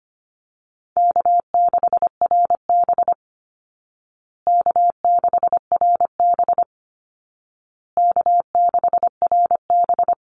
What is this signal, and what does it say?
Morse code, heard as a keyed tone, about 25 wpm, 700 Hz. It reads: K6RB K6RB K6RB
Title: Call Signs Repeated 2x - 25wpm